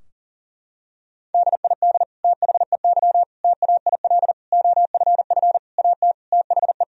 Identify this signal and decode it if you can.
Morse code, heard as a keyed tone, about 37 wpm, 700 Hz. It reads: DID THEY TAIL OFF AT THE
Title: Sentences from Top 1000 Words - Review Part 3 - 40wpm